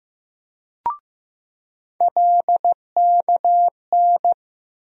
Morse code, heard as a keyed tone, about 15 wpm, 700 Hz. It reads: LKN
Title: USA Call Sign Postfixes - 3 Letters 15wpm